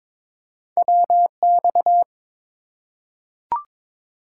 Morse code, heard as a keyed tone, about 22 wpm, 700 Hz. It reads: WX
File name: Top 90 QSO Elements - 2x Character Spacing - 22wpm